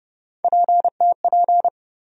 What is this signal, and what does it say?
Morse code, heard as a keyed tone, about 29 wpm, 700 Hz. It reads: PTP
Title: USA Call Sign Postfixes - 3 Letters 30wpm